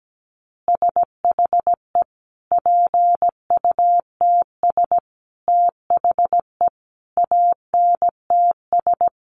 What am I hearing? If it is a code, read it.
Morse code, heard as a keyed tone, about 17 wpm, 700 Hz. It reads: SHE PUTS THE ANTS
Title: Easy Phrases 17wpm